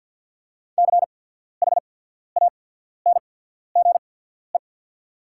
Morse code, heard as a keyed tone, about 48 wpm, spaced as for 14 wpm, 690 Hz. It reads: CHANGE
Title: 8x Character Spacing - Top 200 Words - 50wpm